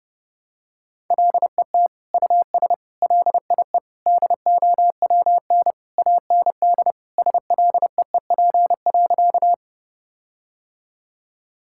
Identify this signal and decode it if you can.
Morse code, heard as a keyed tone, about 32 wpm, 700 Hz. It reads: LET US LIE DOWN AND SLEEP.
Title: Sentences from Top 1600 Words 30wpm